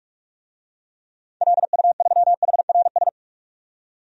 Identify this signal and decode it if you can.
Morse code, heard as a keyed tone, about 44 wpm, 700 Hz. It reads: LU3HRS
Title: DXCC-100 Call Signs - 6 Characters - 45wpm